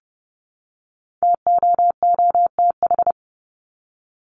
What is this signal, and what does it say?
Morse code, heard as a keyed tone, about 30 wpm, 700 Hz. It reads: TOOTH
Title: Top 1751-2000 Words 30wpm